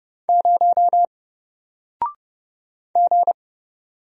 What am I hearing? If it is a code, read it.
Morse code, heard as a keyed tone, about 30 wpm, 700 Hz. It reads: 0 G
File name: Single-Letter-Number-30wpm